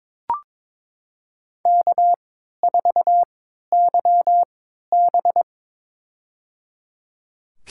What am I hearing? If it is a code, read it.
Morse code, heard as a keyed tone, about 22 wpm, spaced as for 13 wpm, 700 Hz. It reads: K4YB
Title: USA Call Signs - 4 Characters - 3x Character Spacing - 22wpm